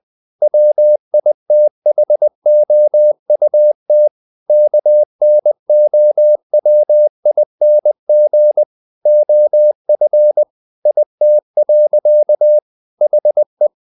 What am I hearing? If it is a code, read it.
Morse code, heard as a keyed tone, about 20 wpm, 600 Hz. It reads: WITHOUT KNOWING OF IT. HE
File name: the old man and the sea part 2 20wpm